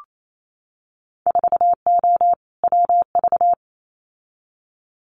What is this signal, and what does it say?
Morse code, heard as a keyed tone, about 28 wpm, 700 Hz. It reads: 4O WV
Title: Winter Field Day Exchanges Encore 28wpm